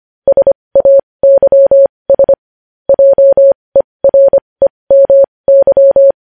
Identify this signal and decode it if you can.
Morse code, heard as a keyed tone, about 25 wpm, spaced as for 20 wpm, 550 Hz. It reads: SAYS JEREMY
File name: podcast / News headlines-1x-25x20